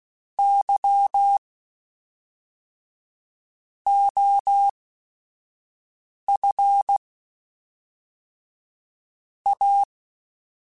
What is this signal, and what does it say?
Morse code, heard as a keyed tone, about 16 wpm, spaced as for 5 wpm, 800 Hz. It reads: Y OF A
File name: Technical Subjects / Technical Subjects-mp3s / Sw Pwr Sup @5